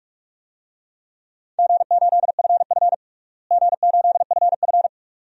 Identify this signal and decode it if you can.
Morse code, heard as a keyed tone, about 47 wpm, 700 Hz. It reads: G8FF G8FF
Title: DXCC-100 Call Signs - 4 Characters - Repeated 2x - 45wpm